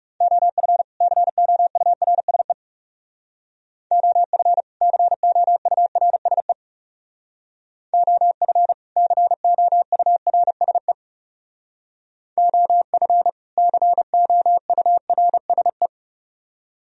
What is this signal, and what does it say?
Morse code, heard as a keyed tone, about 36 wpm, 700 Hz. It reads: OF COURSE OF COURSE OF COURSE OF COURSE
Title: Speed-Racing - Sets of 2 Words 30wpm